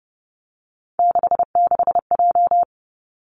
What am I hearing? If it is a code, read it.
Morse code, heard as a keyed tone, about 30 wpm, 700 Hz. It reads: 66J
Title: Three Challenging Letter-Number Combinations 30wpm